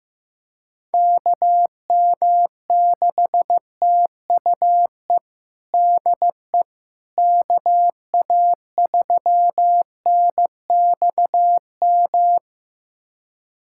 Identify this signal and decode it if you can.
Morse code, heard as a keyed tone, about 15 wpm, 700 Hz. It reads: KM6TUE DE KA3NXM
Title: USA Callsign DE Callsign 15wpm